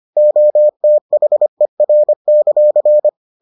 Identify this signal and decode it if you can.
Morse code, heard as a keyed tone, about 25 wpm, 600 Hz. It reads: OTHER;